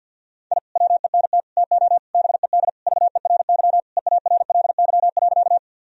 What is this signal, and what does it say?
Morse code, heard as a keyed tone, about 46 wpm, 700 Hz. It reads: I WENT TO BED VERY EARLY.
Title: Sentences from Top 1000 Words - Review Part 2 - 50wpm